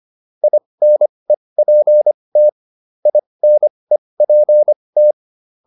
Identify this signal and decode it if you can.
Morse code, heard as a keyed tone, about 25 wpm, spaced as for 20 wpm, 600 Hz. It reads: INEPT INEPT